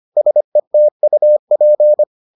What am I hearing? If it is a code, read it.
Morse code, heard as a keyed tone, about 25 wpm, 600 Hz. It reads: SETUP